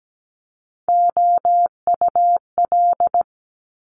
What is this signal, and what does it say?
Morse code, heard as a keyed tone, about 17 wpm, 700 Hz. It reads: OUL
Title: Top Three Letter Combinations in Words 17wpm